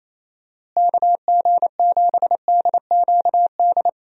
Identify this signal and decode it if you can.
Morse code, heard as a keyed tone, about 28 wpm, 700 Hz. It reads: KG7DQD
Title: USA Call Signs - 6 Characters - Repeated 2x - 28wpm